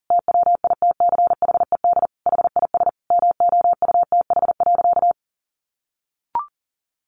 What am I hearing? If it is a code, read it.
Morse code, heard as a keyed tone, about 40 wpm, 700 Hz. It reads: TWITCHED HIS MOUTH.
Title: A Sound Of Thunder 40wpm